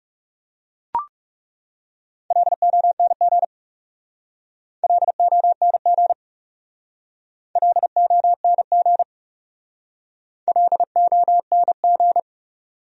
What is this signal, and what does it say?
Morse code, heard as a keyed tone, about 38 wpm, 700 Hz. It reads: LONG LONG LONG LONG